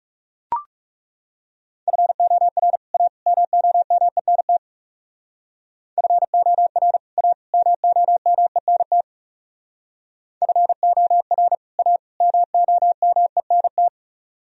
Morse code, heard as a keyed tone, about 38 wpm, 700 Hz. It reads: FOR A MOMENT FOR A MOMENT FOR A MOMENT
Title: Speed-Racing - Sets of 3 Words 30wpm